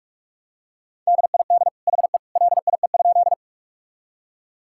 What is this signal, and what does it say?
Morse code, heard as a keyed tone, about 41 wpm, 700 Hz. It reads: DID HE LIE?
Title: Sentences from Top 1600 Words 45wpm